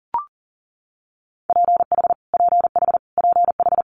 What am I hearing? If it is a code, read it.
Morse code, heard as a keyed tone, about 42 wpm, 700 Hz. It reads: PH PH PH
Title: Multi-letter Phonemes Repeated 3x 40wpm